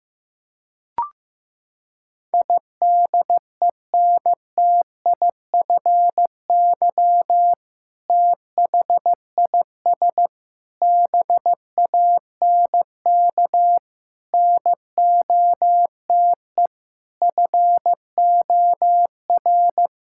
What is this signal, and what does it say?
Morse code, heard as a keyed tone, about 15 wpm, 700 Hz. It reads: IDENTIFY THIS BANK NOTE FOR
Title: Sentences from Top 1000 Words - Review Part 6 - 15wpm